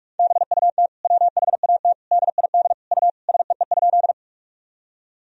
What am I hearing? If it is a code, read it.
Morse code, heard as a keyed tone, about 46 wpm, 700 Hz. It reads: BUT WHAT DID U SEE?
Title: Sentences from Top 100 Words and Abbreviations 45wpm